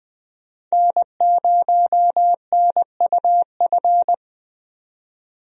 Morse code, heard as a keyed tone, about 20 wpm, 700 Hz. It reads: N0NUF
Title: USA Call Signs - 5 Characters - 20wpm